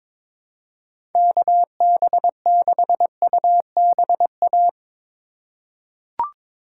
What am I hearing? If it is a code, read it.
Morse code, heard as a keyed tone, about 22 wpm, 700 Hz. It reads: KB6UBA
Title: USA Call Signs - 6 Characters - 2x Character Spacing - 22wpm